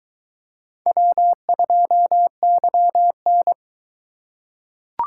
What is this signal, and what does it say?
Morse code, heard as a keyed tone, about 22 wpm, 700 Hz. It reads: W2YN